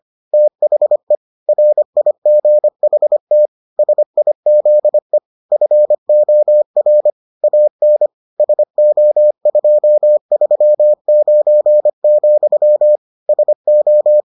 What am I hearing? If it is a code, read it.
Morse code, heard as a keyed tone, about 25 wpm, 600 Hz. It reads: THE RIGHT SIZE FOR AN SO239, SO